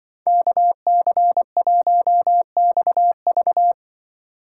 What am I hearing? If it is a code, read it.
Morse code, heard as a keyed tone, about 24 wpm, 700 Hz. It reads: KC1XV
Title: Speed Racing - USA 2x2 Call Signs 20wpm